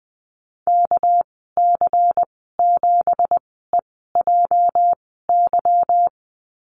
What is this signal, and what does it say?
Morse code, heard as a keyed tone, about 20 wpm, spaced as for 14 wpm, 700 Hz. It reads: KC7EJY
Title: 231 Lesson 37g - Letter x - Call Signs 20wpm